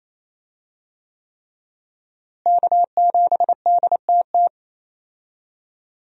Morse code, heard as a keyed tone, about 28 wpm, 700 Hz. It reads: K7DTT
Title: USA Call Signs - 5 Characters - Repeated 3x - 28wpm